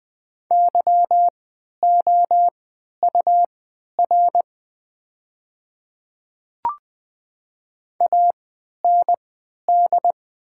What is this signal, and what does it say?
Morse code, heard as a keyed tone, about 20 wpm, spaced as for 11 wpm, 700 Hz. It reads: YOUR AND
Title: 3x Character Spacing - Top 100 Words - 20wpm